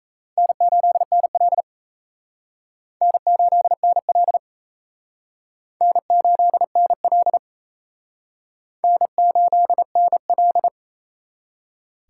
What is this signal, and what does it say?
Morse code, heard as a keyed tone, about 35 wpm, 700 Hz. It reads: N8NL N8NL N8NL N8NL
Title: Speed Racing - USA 1x2 Call Signs 28wpm